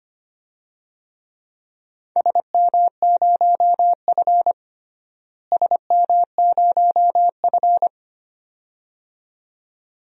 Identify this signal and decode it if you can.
Morse code, heard as a keyed tone, about 25 wpm, 700 Hz. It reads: SM0F SM0F
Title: DXCC-100 Call Signs - 4 Characters - Repeated 2x - 25wpm